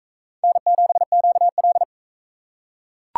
Morse code, heard as a keyed tone, about 38 wpm, 700 Hz. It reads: N7QL